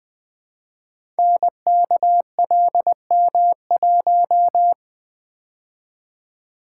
Morse code, heard as a keyed tone, about 20 wpm, 700 Hz. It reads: NKLM1
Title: Code Groups - 5 Letter:Numbers - 20wpm